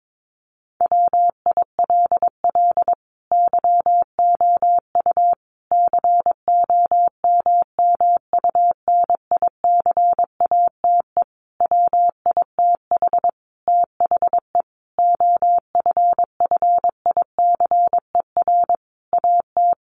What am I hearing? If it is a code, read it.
Morse code, heard as a keyed tone, about 22 wpm, 700 Hz. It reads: WILL YOU COMMUNICATE WITH THE OFFICER AT
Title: Sentences from Top 1300 Words 22wpm